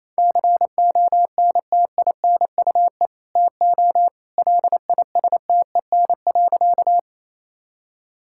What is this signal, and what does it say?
Morse code, heard as a keyed tone, about 28 wpm, 700 Hz. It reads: CONTINUE TO LISTEN.
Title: Sentences from Top 1000 Words - Review Part 6 - 28wpm